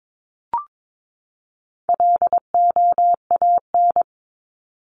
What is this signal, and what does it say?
Morse code, heard as a keyed tone, about 22 wpm, 700 Hz. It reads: LOAN
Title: Top 1501-1750 Words 22wpm